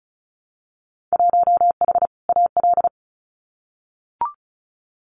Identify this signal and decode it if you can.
Morse code, heard as a keyed tone, about 36 wpm, 700 Hz. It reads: 1H AL
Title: Winter Field Day Exchanges 35wpm